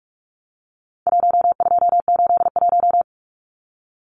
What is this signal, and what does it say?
Morse code, heard as a keyed tone, about 44 wpm, 700 Hz. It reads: 1281
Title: Numbers - 4-Digit 45wpm